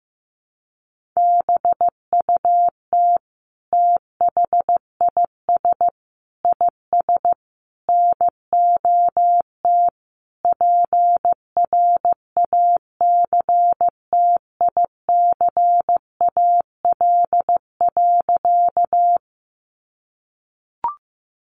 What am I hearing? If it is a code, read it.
Morse code, heard as a keyed tone, about 14 wpm, 700 Hz. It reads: BUT THIS IS NOT PRACTICAL.
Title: Sentences from Top 1400 Words 15wpm